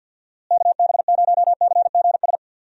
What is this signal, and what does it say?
Morse code, heard as a keyed tone, about 52 wpm, 700 Hz. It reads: KB0CGS
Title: USA Call Signs - 6 Characters - Repeated 2x - 50wpm